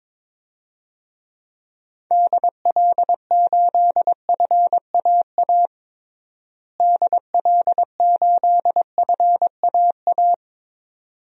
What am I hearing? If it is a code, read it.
Morse code, heard as a keyed tone, about 22 wpm, 700 Hz. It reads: DL8FAA DL8FAA
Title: DXCC-100 Call Signs - 6 Characters - Repeated 2x - 22wpm